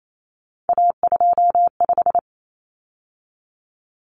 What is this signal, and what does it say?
Morse code, heard as a keyed tone, about 28 wpm, 700 Hz. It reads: A25
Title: Common Cut Numbers - 3-Digit 28wpm